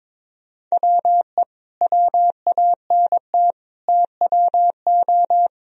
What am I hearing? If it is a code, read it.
Morse code, heard as a keyed tone, about 22 wpm, 700 Hz. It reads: WE WANT TWO
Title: Sentences from Top 1000 Words - Review Part 1 - 22wpm